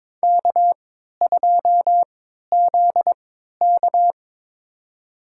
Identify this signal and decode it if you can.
Morse code, heard as a keyed tone, about 21 wpm, spaced as for 12 wpm, 700 Hz. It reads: K2ZK
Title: USA Call Signs - 4 Characters - 3x Character Spacing - 22wpm